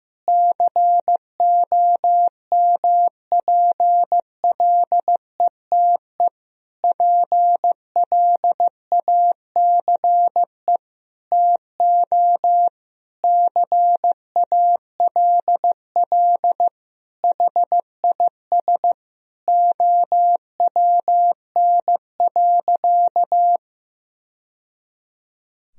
Morse code, heard as a keyed tone, about 15 wpm, 700 Hz. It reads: COMPLETE PLACE TO CALL HIS OWN.
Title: Sentences from Top 1000 Words - Review Part 2 - 15wpm